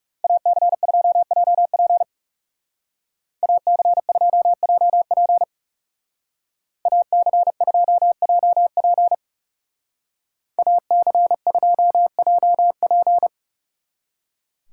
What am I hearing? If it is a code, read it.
Morse code, heard as a keyed tone, about 37 wpm, 700 Hz. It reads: AC2JP AC2JP AC2JP AC2JP
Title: Speed Racing - USA 2x2 Call Signs 30wpm